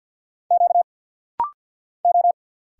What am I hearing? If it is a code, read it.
Morse code, heard as a keyed tone, about 47 wpm, 700 Hz. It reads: Q O
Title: Single Letter-Number Alt - Rapid-Fire 50wpm